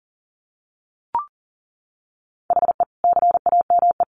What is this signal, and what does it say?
Morse code, heard as a keyed tone, about 40 wpm, 700 Hz. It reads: HE CAME
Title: Sentences from Top 1100 Words 40wpm